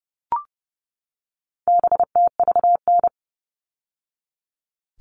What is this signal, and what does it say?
Morse code, heard as a keyed tone, about 30 wpm, 700 Hz. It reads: BTVN
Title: All Cut Numbers - 4-Digit 30wpm